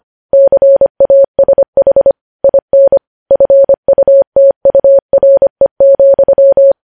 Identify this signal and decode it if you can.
Morse code, heard as a keyed tone, about 25 wpm, 550 Hz. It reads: CASH IN FUTURE,
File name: podcast / News headlines-1x-25x25